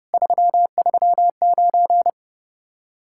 Morse code, heard as a keyed tone, about 30 wpm, 700 Hz. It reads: 339